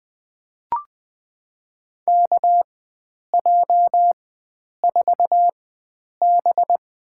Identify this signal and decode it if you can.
Morse code, heard as a keyed tone, about 20 wpm, spaced as for 9 wpm, 700 Hz. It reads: KJ4B
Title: USA Call Signs - 4x Character Spacing - 20wpm